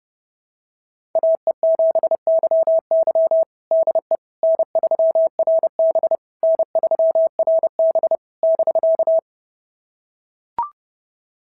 Morse code, heard as a keyed tone, about 26 wpm, 650 Hz. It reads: AE7YY DE N3RB N3RB <BK>
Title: SOTA QSO Practice 30wpm